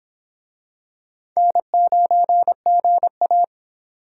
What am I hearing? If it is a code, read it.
Morse code, heard as a keyed tone, about 22 wpm, 700 Hz. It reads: N9GA